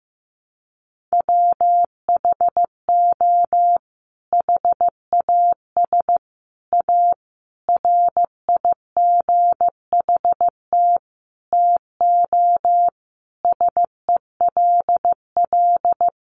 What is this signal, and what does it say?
Morse code, heard as a keyed tone, about 15 wpm, 700 Hz. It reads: WHO HAS A RIGHT TO SELL
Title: Sentences from Top 1100 Words 15wpm